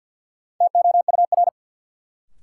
Morse code, heard as a keyed tone, about 49 wpm, 700 Hz. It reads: TOUR
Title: Sentences from Top 1400 Words - New Word Drill - 50wpm